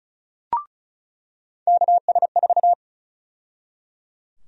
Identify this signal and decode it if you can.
Morse code, heard as a keyed tone, about 34 wpm, 700 Hz. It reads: KS4
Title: USA Call Sign Prefixes - 35wpm